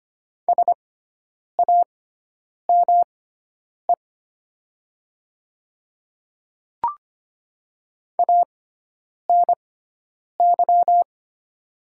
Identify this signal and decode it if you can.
Morse code, heard as a keyed tone, about 25 wpm, spaced as for 9 wpm, 700 Hz. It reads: SAME ANY
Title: 6x Character Spacing - Top 200 Words - 25wpm